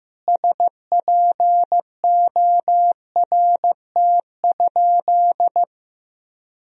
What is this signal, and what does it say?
Morse code, heard as a keyed tone, about 15 wpm, 700 Hz. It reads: SPORT?
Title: Sentences from Top 1000 Words - Review Part 4 - 15wpm